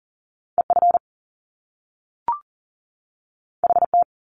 Morse code, heard as a keyed tone, about 40 wpm, 700 Hz. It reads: EF HT
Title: Top Two Letter Combinations in Words 40wpm